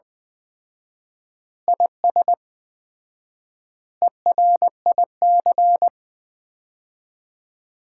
Morse code, heard as a keyed tone, about 21 wpm, 700 Hz. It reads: IS ERIC
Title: Common QSO Phrases - 4x Word-Spacing - 20wpm